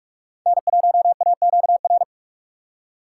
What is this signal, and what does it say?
Morse code, heard as a keyed tone, about 46 wpm, 700 Hz. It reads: N1AQR
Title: USA Call Signs - 5 Characters - 6x Character Spacing - 45wpm